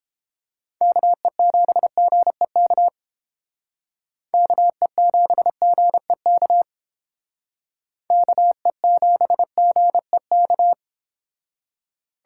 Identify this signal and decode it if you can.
Morse code, heard as a keyed tone, about 29 wpm, 690 Hz. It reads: KE7GEK KE7GEK KE7GEK